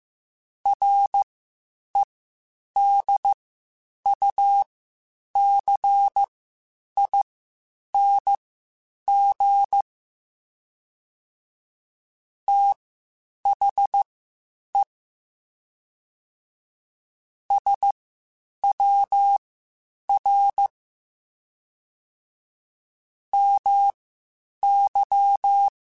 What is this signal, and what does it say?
Morse code, heard as a keyed tone, about 15 wpm, spaced as for 8 wpm, 800 Hz. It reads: REDUCING THE SWR MY
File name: (spaced) / QSOs Spaced-mp3s / WD0FBT 1 @13 sp